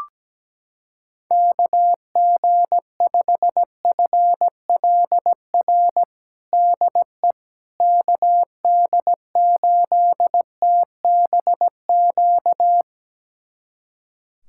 Morse code, heard as a keyed tone, about 17 wpm, 700 Hz. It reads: KG5FLR DE KD8TBQ
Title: USA Callsign DE Callsign 17wpm